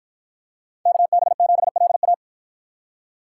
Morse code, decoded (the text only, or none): KB7LA